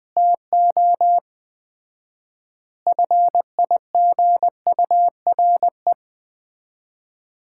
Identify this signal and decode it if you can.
Morse code, heard as a keyed tone, about 20 wpm, 700 Hz. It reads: TO FIGURE
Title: 4x Word-Spacing - Sets of 4 Words - 20wpm